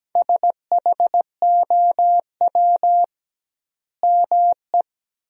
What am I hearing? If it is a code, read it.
Morse code, heard as a keyed tone, about 17 wpm, 700 Hz. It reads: SHOW ME
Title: 2x Word-Spacing - Sentences from Top 200 Words - 17wpm